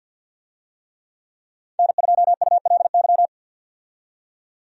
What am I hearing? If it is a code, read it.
Morse code, heard as a keyed tone, about 52 wpm, 700 Hz. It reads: N2ULY
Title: USA Call Signs - 5 Characters - 50wpm